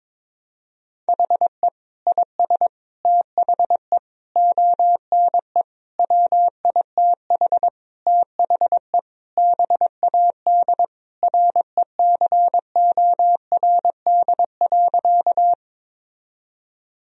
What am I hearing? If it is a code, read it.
Morse code, heard as a keyed tone, about 22 wpm, 700 Hz. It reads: HE IS THE ONE WITH THE BAD RECORD.
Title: Sentences from Top 1000 Words - Review Part 3 - 22wpm